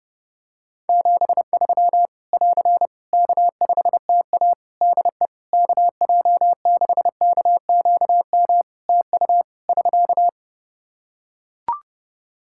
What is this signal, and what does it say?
Morse code, heard as a keyed tone, about 30 wpm, 700 Hz. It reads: 73 <AR> K5TA DE KJ6KQM TU <SK>
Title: Practice QSO - Exchange 4 - Ending - 30wpm